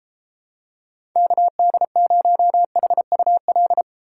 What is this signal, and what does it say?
Morse code, heard as a keyed tone, about 30 wpm, 700 Hz. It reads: KD0HUL